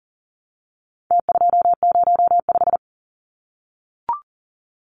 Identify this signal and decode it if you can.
Morse code, heard as a keyed tone, about 40 wpm, 700 Hz. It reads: T205
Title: Numbers - Contest Style - 4-Digit 40wpm